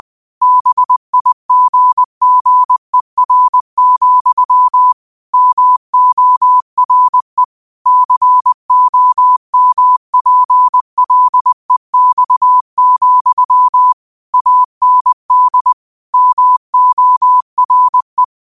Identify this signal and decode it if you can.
Morse code, heard as a keyed tone, about 20 wpm, 1000 Hz. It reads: BIGGER, MORE COMPLEX, AND MORE
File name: qotdfiles / 20wpm